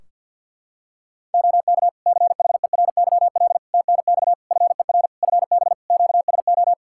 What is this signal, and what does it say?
Morse code, heard as a keyed tone, about 51 wpm, 700 Hz. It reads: OK CHERYL TNX FER FB QSO
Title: Practice QSO - Exchange 4 - Ending - 50wpm